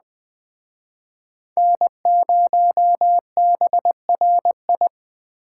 Morse code, heard as a keyed tone, about 20 wpm, 700 Hz. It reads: N0BRI
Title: Speed Racing - USA Call Signs 17wpm